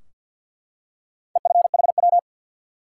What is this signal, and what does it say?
Morse code, heard as a keyed tone, about 52 wpm, 700 Hz. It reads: EFHW
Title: Top 180 QSO Elements 50wpm